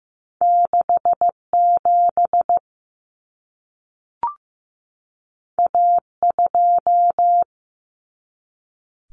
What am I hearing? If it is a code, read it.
Morse code, heard as a keyed tone, about 15 wpm, 700 Hz. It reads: 67 A2
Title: Common Cut Numbers - 2-Digit 15wpm